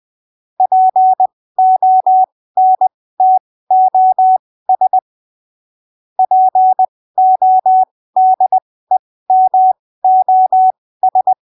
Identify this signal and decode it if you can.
Morse code, heard as a keyed tone, about 20 wpm, spaced as for 15 wpm, 750 Hz. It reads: PONTOS PODEMOS
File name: arquivos / texto / texto 04 20ppm